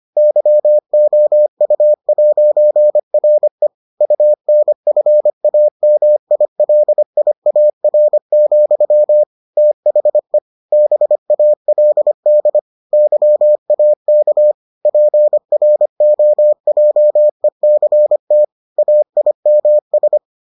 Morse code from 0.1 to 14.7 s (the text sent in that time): YOU'RE UNFAMILIAR, THE BALD YAK